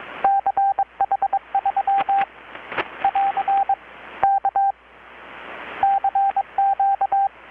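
Morse code, heard as a keyed tone, about 22 wpm, 790 Hz, with radio noise. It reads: CH3 <AR> K CQ
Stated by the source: Morse from KPH's Night of Nights XX maritime radiotelegraphy event, received in Chiapas, Mexico about 3,800 km away on a Kenwood R-600 with a grounded 10 m long-wire antenna.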